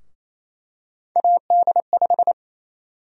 Morse code, heard as a keyed tone, about 28 wpm, 700 Hz. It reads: AD5
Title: USA Call Sign Prefixes - 28wpm